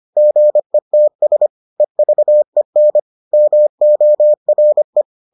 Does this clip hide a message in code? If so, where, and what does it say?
Yes: Morse code from 0.2 to 5.0 s, heard as a keyed tone, about 25 wpm, 600 Hz. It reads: GETS EVEN MORE